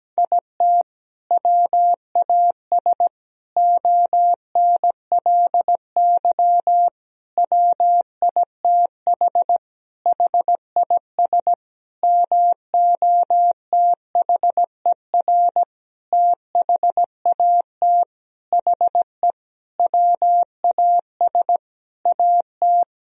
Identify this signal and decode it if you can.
Morse code, heard as a keyed tone, about 17 wpm, 700 Hz. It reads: IT WAS ONLY WITH HIS MOTHER THAT HE WAS AT
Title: Sentences from Top 1000 Words - Review Part 1 - 17wpm